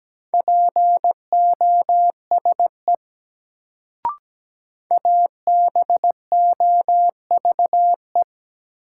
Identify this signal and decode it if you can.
Morse code, heard as a keyed tone, about 17 wpm, 700 Hz. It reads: POSE ABOVE
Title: Sentences from Top 1000 Words - Review Part 2 - New Word Drill - 17wpm